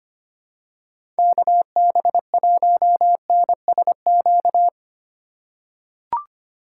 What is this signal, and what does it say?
Morse code, heard as a keyed tone, about 25 wpm, 700 Hz. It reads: KB1NSQ
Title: USA Call Signs - 6 Characters - 2x Character Spacing - 25wpm